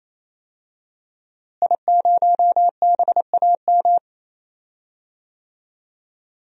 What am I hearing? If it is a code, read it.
Morse code, heard as a keyed tone, about 28 wpm, 700 Hz. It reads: I0BAM
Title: DXCC-100 Call Signs - 5 Characters - Repeated 3x - 28wpm